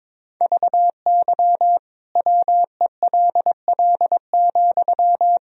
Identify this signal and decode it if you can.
Morse code, heard as a keyed tone, about 22 wpm, 700 Hz. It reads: VY WELL,
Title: Sentences from Top 200 Words and Abbreviations 22wpm